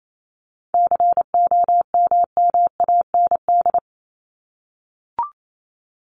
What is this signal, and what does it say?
Morse code, heard as a keyed tone, about 28 wpm, 700 Hz. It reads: COMMAND
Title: Top 1501-1750 Words 28wpm